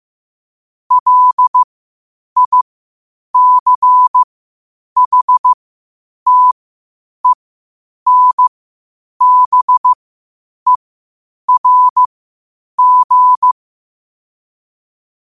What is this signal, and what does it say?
Morse code, heard as a keyed tone, about 15 wpm, spaced as for 8 wpm, 1000 Hz. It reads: LICHTENBERG
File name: qotdfiles / 8wpm